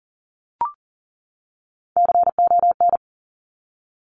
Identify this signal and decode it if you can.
Morse code, heard as a keyed tone, about 40 wpm, 700 Hz. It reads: CON
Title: Top Three Letter Combinations in Words 40wpm